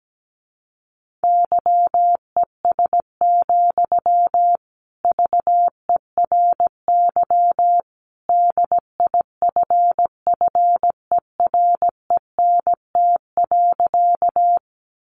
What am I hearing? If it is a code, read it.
Morse code, heard as a keyed tone, about 17 wpm, 700 Hz. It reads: YES, VERY DIFFERENT.
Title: Sentences from Top 600 Words 17wpm